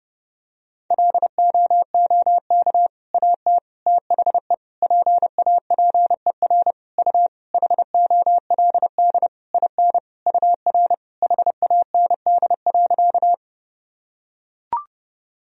Morse code, heard as a keyed tone, about 28 wpm, 700 Hz. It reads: LOOK AT THE PAPER U HOLD IN UR HAND.
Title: Sentences from Top 400 Words and Abbreviations 30wpm